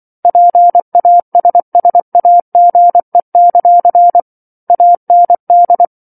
Morse code, heard as a keyed tone, about 24 wpm, 700 Hz. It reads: PASSAGE; AND
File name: 07-Chapter-07 24 WPM